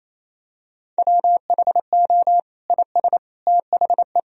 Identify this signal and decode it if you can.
Morse code, heard as a keyed tone, about 28 wpm, 700 Hz. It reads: WHO IS THE
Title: Sentences from Top 1000 Words - Review Part 3 - 28wpm